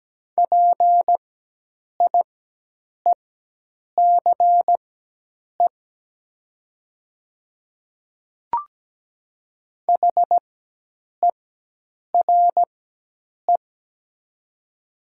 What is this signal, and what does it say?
Morse code, heard as a keyed tone, about 17 wpm, spaced as for 8 wpm, 700 Hz. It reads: PIECE HERE
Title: 4x Character Spacing - Top 500 Words - 17wpm